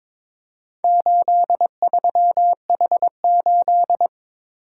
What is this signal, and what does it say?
Morse code, heard as a keyed tone, about 22 wpm, 700 Hz. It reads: 83H8
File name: practice / 22wpm / Four Letter-Number Combinations - 22wpm